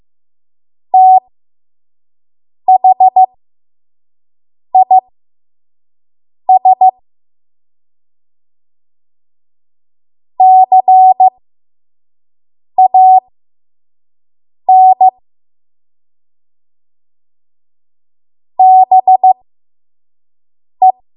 Morse code, heard as a keyed tone, about 15 wpm, spaced as for 5 wpm, 750 Hz. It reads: THIS CAN BE
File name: Morse / 05WPM